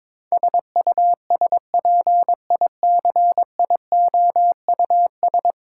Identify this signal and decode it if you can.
Morse code, heard as a keyed tone, about 22 wpm, 700 Hz. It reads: SUSPICIOUS
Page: morse-code-ninja-practice-22wpm